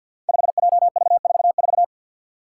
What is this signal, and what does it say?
Morse code, heard as a keyed tone, about 46 wpm, 700 Hz. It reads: 5JV44
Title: Five Challenging Letter-Number Combinations 50wpm